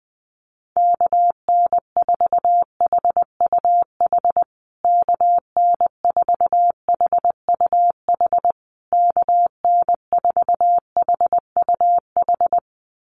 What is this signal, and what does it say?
Morse code, heard as a keyed tone, about 20 wpm, 700 Hz. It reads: KN4HUH KN4HUH KN4HUH
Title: USA Call Signs Repeated 3x 20wpm